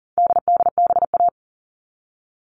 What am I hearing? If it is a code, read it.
Morse code, heard as a keyed tone, about 40 wpm, 700 Hz. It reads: DDBA
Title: All Cut Numbers - 4-Digit 40wpm